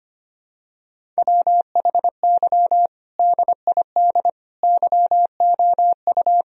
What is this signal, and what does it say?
Morse code, heard as a keyed tone, about 25 wpm, 700 Hz. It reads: WHY DID YOU
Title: Sentences from Top 1000 Words - Review Part 1 - 25wpm